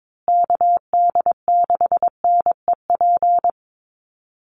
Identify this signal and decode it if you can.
Morse code, heard as a keyed tone, about 22 wpm, 700 Hz. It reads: KD6NEP
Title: USA Call Signs Repeated 3x 22wpm